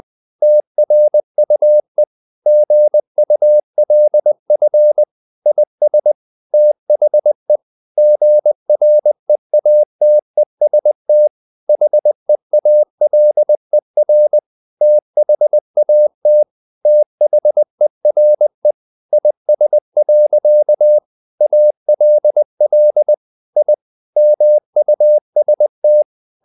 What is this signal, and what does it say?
Morse code, heard as a keyed tone, about 20 wpm, 600 Hz. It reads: TRUE GULF IS THE GREATEST HEALER THAT THERE IS. ALL I MUST
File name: the old man and the sea part 2 20wpm